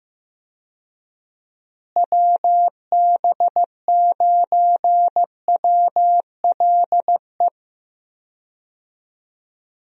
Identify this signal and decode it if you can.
Morse code, heard as a keyed tone, about 15 wpm, 700 Hz. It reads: WB9WLE
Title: USA Call Signs - 6 Characters - Repeated 3x - 15wpm